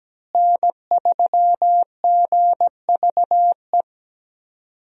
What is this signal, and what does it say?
Morse code, heard as a keyed tone, about 17 wpm, 700 Hz. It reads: N3GVE
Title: USA Call Signs - 5 Characters - Repeated 3x - 17wpm